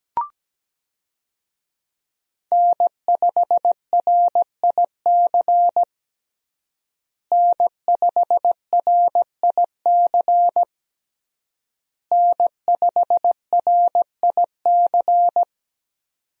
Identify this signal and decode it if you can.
Morse code, heard as a keyed tone, about 17 wpm, 700 Hz. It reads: N5RIC N5RIC N5RIC
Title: USA Call Signs - 5 Characters - Repeated 3x - 17wpm